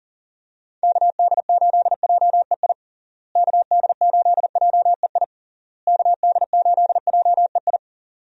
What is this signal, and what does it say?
Morse code, heard as a keyed tone, about 37 wpm, 700 Hz. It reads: KD8JEI KD8JEI KD8JEI
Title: USA Call Signs - 6 Characters - Repeated 3x - 40wpm